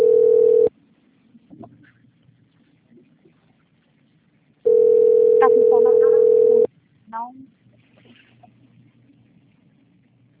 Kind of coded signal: telephone call progress tone

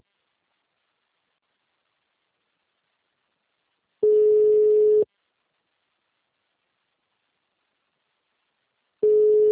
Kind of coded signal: telephone call progress tone

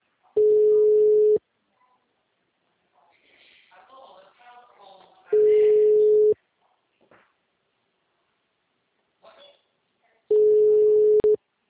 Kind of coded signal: telephone call progress tone

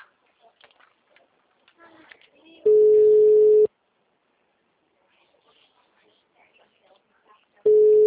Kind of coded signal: telephone call progress tone

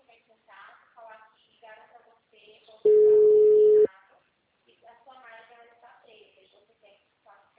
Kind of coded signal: telephone call progress tone

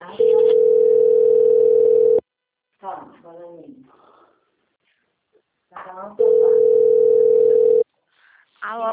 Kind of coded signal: telephone call progress tone

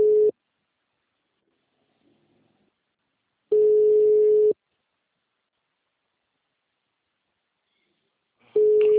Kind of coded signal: telephone call progress tone

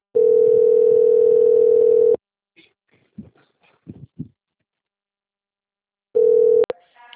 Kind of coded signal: telephone call progress tone

